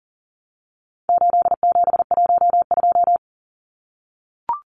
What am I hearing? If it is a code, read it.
Morse code, heard as a keyed tone, about 40 wpm, 700 Hz. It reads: 8712
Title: Numbers - 4-Digit 40wpm